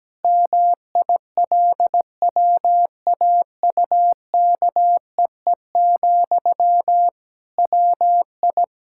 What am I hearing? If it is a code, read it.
Morse code, heard as a keyed tone, about 17 wpm, 700 Hz. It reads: MILWAUKEE, WI
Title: Practice QSO - Exchange 1 - Beginning - 17wpm